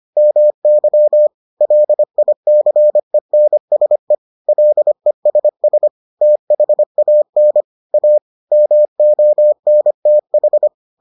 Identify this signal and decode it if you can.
Morse code, heard as a keyed tone, about 25 wpm, 600 Hz. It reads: MY LICENSE LESS THAN A MONTH